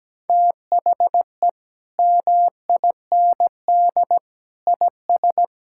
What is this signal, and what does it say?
Morse code, heard as a keyed tone, about 17 wpm, 700 Hz. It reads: THE MIND IS
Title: Sentences from Top 1000 Words - Review Part 6 - 17wpm